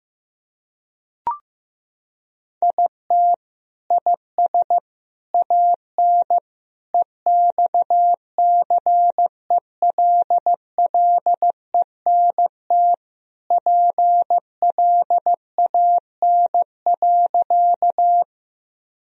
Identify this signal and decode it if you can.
Morse code, heard as a keyed tone, about 15 wpm, 700 Hz. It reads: IT IS AN EXCELLENT PLAN.
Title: Sentences from Top 1000 Words - Review Part 2 - 15wpm